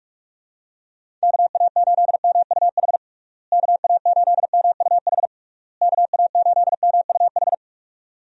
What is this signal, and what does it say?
Morse code, heard as a keyed tone, about 46 wpm, 700 Hz. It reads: KA8MUH KA8MUH KA8MUH
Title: USA Call Signs - 6 Characters - Repeated 3x - 45wpm